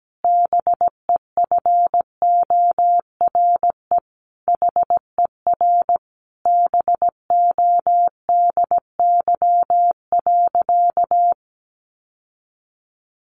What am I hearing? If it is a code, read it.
Morse code, heard as a keyed tone, about 17 wpm, 700 Hz. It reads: BEFORE HER BODY.
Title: Sentences from Top 1500 Words 17wpm